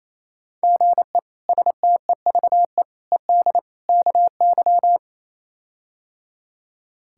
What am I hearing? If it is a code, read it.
Morse code, heard as a keyed tone, about 27 wpm, 700 Hz. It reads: GE STEVE ED KY
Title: K1USN SST QSO Practice - 28wpm